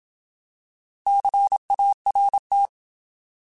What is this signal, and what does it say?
Morse code, heard as a keyed tone, about 27 wpm, 800 Hz. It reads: CART
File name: Alphabetical